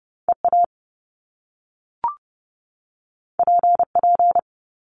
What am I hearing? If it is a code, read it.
Morse code, heard as a keyed tone, about 30 wpm, 700 Hz. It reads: EA PP
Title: Top Two Letter Combinations in Words 30wpm